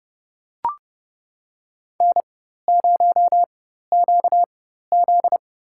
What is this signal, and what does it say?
Morse code, heard as a keyed tone, about 31 wpm, spaced as for 14 wpm, 700 Hz. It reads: N0QZ
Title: USA Call Signs - 4 Characters - 4x Character Spacing - 30wpm